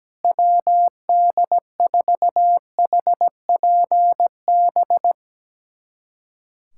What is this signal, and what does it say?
Morse code, heard as a keyed tone, about 17 wpm, 700 Hz. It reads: WD4HPB
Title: USA Callsign DE Callsign - 6 Characters - 17wpm